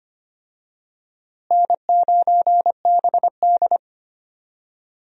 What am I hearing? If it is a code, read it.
Morse code, heard as a keyed tone, about 25 wpm, 700 Hz. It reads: N9BD
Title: USA Call Signs - 4 Characters - Repeated 2x - 25wpm